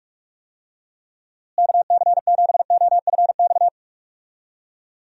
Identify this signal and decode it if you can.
Morse code, heard as a keyed tone, about 46 wpm, 700 Hz. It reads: KC7OFX
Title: USA Call Signs - 6 Characters - Repeated 3x - 45wpm